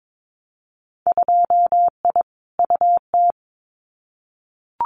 Morse code, heard as a keyed tone, about 22 wpm, 700 Hz. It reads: 2I UT
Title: Winter Field Day Exchanges 22wpm